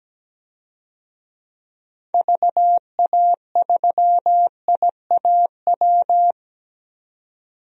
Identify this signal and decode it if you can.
Morse code, heard as a keyed tone, about 17 wpm, 700 Hz. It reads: VA3IAW
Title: DXCC-100 Call Signs - 6 Characters - Repeated 3x - 17wpm